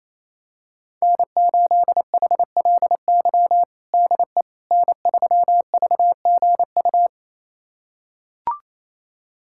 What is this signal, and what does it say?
Morse code, heard as a keyed tone, about 28 wpm, 700 Hz. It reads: N8HLY DE N3VGU
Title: USA Callsign DE Callsign - 5 Characters - 28wpm